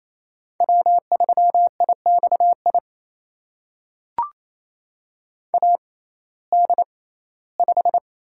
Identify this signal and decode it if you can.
Morse code, heard as a keyed tone, about 28 wpm, spaced as for 17 wpm, 700 Hz. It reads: W3IXI AD5
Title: USA Call Signs - 5 Characters - 6x Character Spacing - 28wpm